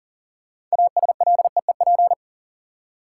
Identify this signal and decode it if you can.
Morse code, heard as a keyed tone, about 40 wpm, 700 Hz. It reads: ASLEEP
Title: Six Letter Words - 40wpm